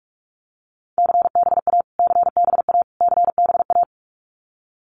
Morse code, heard as a keyed tone, about 44 wpm, 700 Hz. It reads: CBA CBA CBA
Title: Abbreviations Repeated 3x - 45wpm